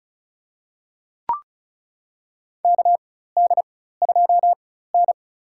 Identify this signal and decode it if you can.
Morse code, heard as a keyed tone, about 36 wpm, spaced as for 16 wpm, 700 Hz. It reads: KD2N
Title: USA Call Signs - 4x Character Spacing - 35wpm